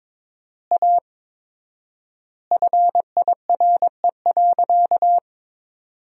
Morse code, heard as a keyed tone, about 22 wpm, 700 Hz. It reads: A FIRE.
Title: 4x Word-Spacing - Sentences from Top 400 Words - 22wpm